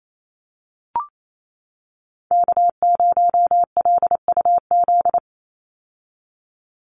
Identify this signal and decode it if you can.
Morse code, heard as a keyed tone, about 28 wpm, 700 Hz. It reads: K0LUZ
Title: Call Signs - 28wpm